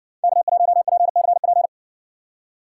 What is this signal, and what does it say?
Morse code, heard as a keyed tone, about 60 wpm, 700 Hz. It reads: K1PCP